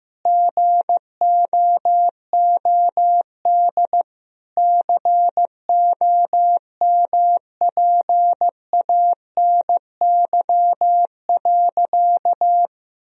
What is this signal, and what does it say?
Morse code, heard as a keyed tone, about 15 wpm, 700 Hz. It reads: GOOD COMPANY.
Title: Sentences from Top 1000 Words - Review Part 3 - 15wpm